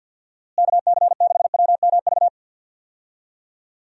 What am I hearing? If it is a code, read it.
Morse code, heard as a keyed tone, about 52 wpm, 700 Hz. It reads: KC6WMV
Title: USA Call Signs - 6 Characters - Repeated 3x - 50wpm